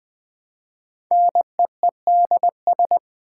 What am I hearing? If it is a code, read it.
Morse code, heard as a keyed tone, about 21 wpm, 700 Hz. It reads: NEEDS
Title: 4x Word-Spacing - Sets of 3 Words - 20wpm